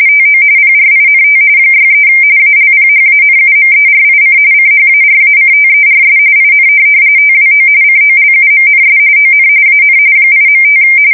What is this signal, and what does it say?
RTTY, about 45.45 baud. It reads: TO PRINT 63 ALPHANUMERIC CHARACTERS IN TWO COLUMNS ON AN 8.5
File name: fsk170 / teletype-corp